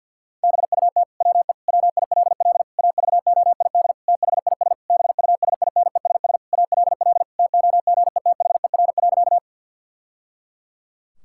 Morse code, heard as a keyed tone, about 51 wpm, 700 Hz. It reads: BUT WE WILL AVOID THIS BUSINESS ALL TOGETHER.
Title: Sentences from Top 1000 Words - Review Part 6 - 50wpm